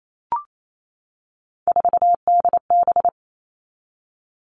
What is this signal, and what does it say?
Morse code, heard as a keyed tone, about 28 wpm, 700 Hz. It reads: 4DB
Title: All Cut Numbers - 3-Digit 28wpm